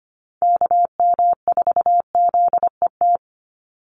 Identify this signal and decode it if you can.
Morse code, heard as a keyed tone, about 25 wpm, 700 Hz. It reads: KM4ZET
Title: USA Call Signs 25wpm